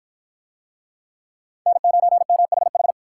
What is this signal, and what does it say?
Morse code, heard as a keyed tone, about 53 wpm, 700 Hz. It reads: N9MHH